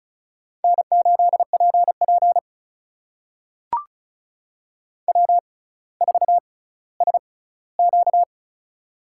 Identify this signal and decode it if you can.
Morse code, heard as a keyed tone, about 36 wpm, spaced as for 18 wpm, 700 Hz. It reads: N8PP W4SQ
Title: USA Call Signs - 4 Characters - 6x Character Spacing - 35wpm